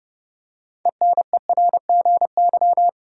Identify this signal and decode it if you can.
Morse code, heard as a keyed tone, about 30 wpm, 700 Hz. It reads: ENERGY
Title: Six Letter Words - 30wpm